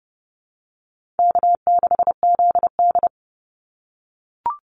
Morse code, heard as a keyed tone, about 30 wpm, 700 Hz. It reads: K6ZD
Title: Call Signs - 30wpm